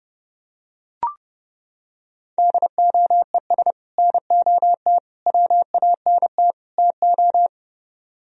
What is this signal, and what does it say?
Morse code, heard as a keyed tone, about 29 wpm, 700 Hz. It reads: DOES NOT WANT TO